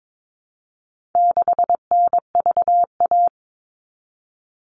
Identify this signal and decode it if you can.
Morse code, heard as a keyed tone, about 22 wpm, 700 Hz. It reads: 6NVA
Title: All Cut Numbers - 4-Digit 22wpm